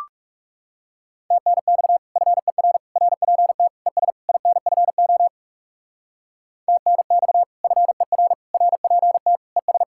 Morse code, heard as a keyed tone, about 38 wpm, 700 Hz. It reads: TNX FER RPT ES INFO TNX FER RPT ES